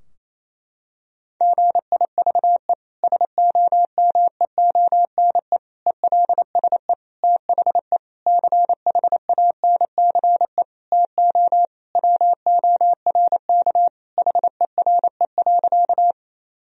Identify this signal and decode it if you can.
Morse code, heard as a keyed tone, about 28 wpm, 700 Hz. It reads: GIVE SOMEONE ELSE THE CHANCE TO WORK HERE.
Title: Sentences from Top 1000 Words - Review Part 4 - 28wpm